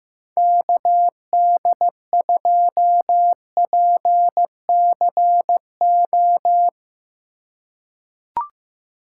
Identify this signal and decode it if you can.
Morse code, heard as a keyed tone, about 15 wpm, 700 Hz. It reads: KD2PCO
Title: USA Call Signs - 4x Character Spacing - 15wpm